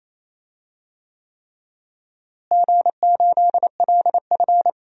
Morse code, heard as a keyed tone, about 28 wpm, 700 Hz. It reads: G8LF
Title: DXCC-100 Call Signs - 4 Characters - Repeated 3x - 28wpm